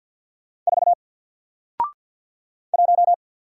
Morse code, heard as a keyed tone, about 49 wpm, 700 Hz. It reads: 4 1
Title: Single Letter-Number Alt 50wpm